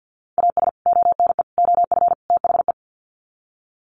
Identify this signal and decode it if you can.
Morse code, heard as a keyed tone, about 50 wpm, 700 Hz. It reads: AS ONE OF THE
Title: Sets of 4 Words 50wpm